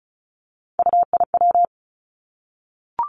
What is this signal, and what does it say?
Morse code, heard as a keyed tone, about 36 wpm, 700 Hz. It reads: UIW
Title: Three Challenging Letter-Number Combinations 35wpm